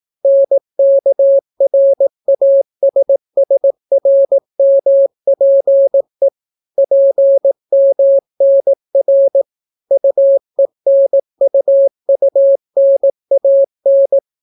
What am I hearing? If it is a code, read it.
Morse code, heard as a keyed tone, about 18 wpm, 550 Hz. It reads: NKRASSRMPE PMNR UENUUNAN